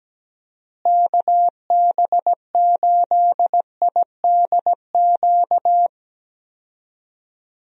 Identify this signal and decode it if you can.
Morse code, heard as a keyed tone, about 17 wpm, 700 Hz. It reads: KB8IDQ
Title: USA Call Signs - 6 Characters - Repeated 3x - 17wpm